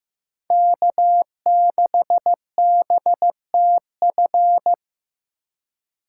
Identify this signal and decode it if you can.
Morse code, heard as a keyed tone, about 18 wpm, 700 Hz. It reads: K6BTF
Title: Speed Racing - USA 1x3 Call Signs 15wpm